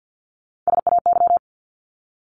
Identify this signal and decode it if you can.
Morse code, heard as a keyed tone, about 49 wpm, 700 Hz. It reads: SAY
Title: ING Suffix - Top 100 Words with Base Words 50wpm